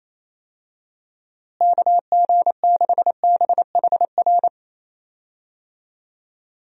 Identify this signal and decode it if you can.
Morse code, heard as a keyed tone, about 28 wpm, 700 Hz. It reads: KG6BHR
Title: USA Call Signs - 6 Characters - Repeated 3x - 28wpm